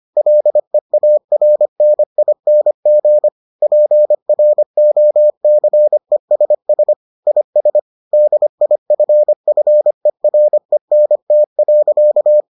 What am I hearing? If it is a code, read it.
Morse code, heard as a keyed tone, about 25 wpm, 600 Hz. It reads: LEARNING PROCESS IS DIFFERENT.